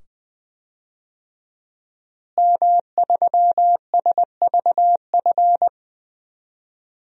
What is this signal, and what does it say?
Morse code, heard as a keyed tone, about 20 wpm, 700 Hz. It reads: M3SVF
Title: DXCC-100 Call Signs - 5 Characters - Repeated 2x - 20wpm